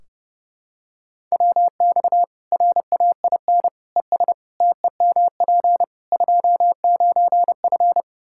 Morse code, heard as a keyed tone, about 30 wpm, 700 Hz. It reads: WX RAIN ES TEMP 29F
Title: Practice QSO - Exchange 2 - Info - 30wpm